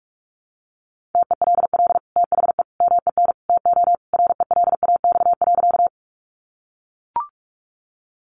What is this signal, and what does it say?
Morse code, heard as a keyed tone, about 44 wpm, 700 Hz. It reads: TELL THE MEN TO RELAX.
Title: Sentences from Top 1400 Words 45wpm